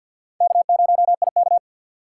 Morse code, heard as a keyed tone, about 53 wpm, 700 Hz. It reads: K0IK
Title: USA Call Signs - 2x Character Spacing - 50wpm